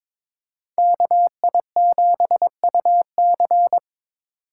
Morse code, heard as a keyed tone, about 26 wpm, 700 Hz. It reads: KI7UC
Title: Speed Racing - USA 2x2 Call Signs 22wpm